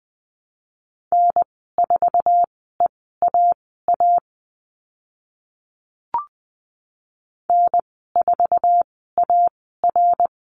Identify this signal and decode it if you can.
Morse code, heard as a keyed tone, about 20 wpm, spaced as for 14 wpm, 700 Hz. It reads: N4EAA N4AR
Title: 076 Lesson 13g - Letter L - Call Signs 20wpm